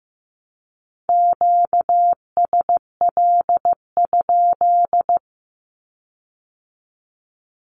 Morse code, heard as a keyed tone, about 15 wpm, 700 Hz. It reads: QSL?
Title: Q-Codes with Spoken Meaning - 15wpm